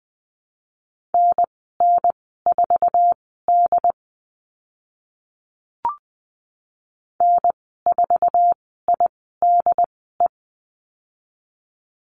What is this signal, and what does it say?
Morse code, heard as a keyed tone, about 20 wpm, spaced as for 14 wpm, 700 Hz. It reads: NN4D N4IDE
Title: 082 Lesson 14g - Number 2 - Call Signs 20wpm